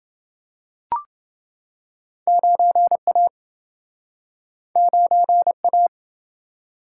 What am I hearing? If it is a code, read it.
Morse code, heard as a keyed tone, about 29 wpm, 700 Hz. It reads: 9A 9A